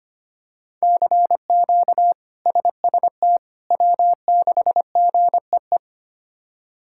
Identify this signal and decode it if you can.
Morse code, heard as a keyed tone, about 25 wpm, 700 Hz. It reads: CQ SST W6GEE
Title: K1USN SST QSO Practice - 25wpm